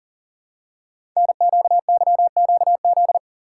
Morse code, heard as a keyed tone, about 41 wpm, 700 Hz. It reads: NQYQZ
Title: Code Groups - 5 Letters - 40wpm